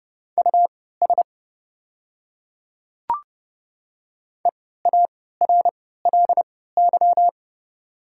Morse code, heard as a keyed tone, about 30 wpm, spaced as for 17 wpm, 700 Hz. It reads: US EARLY
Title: 3x Character Spacing - Top 500 Words - 30wpm